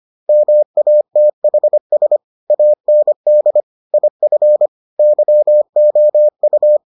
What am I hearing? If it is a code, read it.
Morse code, heard as a keyed tone, about 25 wpm, 600 Hz. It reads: MATHS AND IF YOU